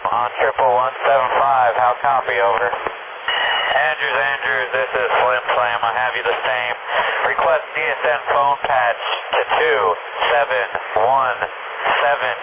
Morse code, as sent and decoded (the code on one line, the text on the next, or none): none